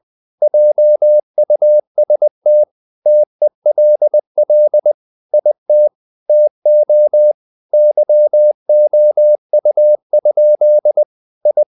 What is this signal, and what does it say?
Morse code, heard as a keyed tone, about 20 wpm, 600 Hz. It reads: JUST TELL IT TO YOU? I
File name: the old man and the sea / the old man and the sea part 1 20wpm